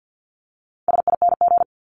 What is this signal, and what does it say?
Morse code, heard as a keyed tone, about 50 wpm, 700 Hz. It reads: SING
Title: Top 500 Words Repeated 50wpm